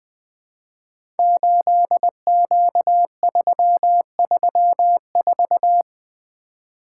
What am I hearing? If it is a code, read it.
Morse code, heard as a keyed tone, about 20 wpm, 700 Hz. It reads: 8Q334
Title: Code Groups - 5 Letter:Numbers - 20wpm